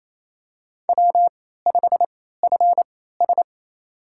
Morse code, heard as a keyed tone, about 28 wpm, spaced as for 16 wpm, 700 Hz. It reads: W5FS
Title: USA Call Signs - 4 Characters - 3x Character Spacing - 28wpm